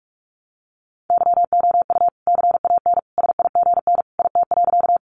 Morse code, heard as a keyed tone, about 44 wpm, 700 Hz. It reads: YOU CAN SIGN IT.
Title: Sentences from Top 1000 Words 45wpm